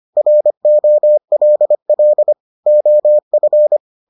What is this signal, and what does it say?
Morse code, heard as a keyed tone, about 25 wpm, 600 Hz. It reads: ROLL OF